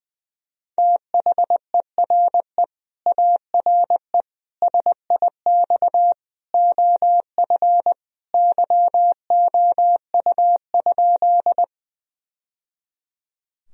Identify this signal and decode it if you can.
Morse code, heard as a keyed tone, about 21 wpm, 700 Hz. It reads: THERE ARE SIX OF YOU?
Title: Sentences from Top 1000 Words - Review Part 2 - 20wpm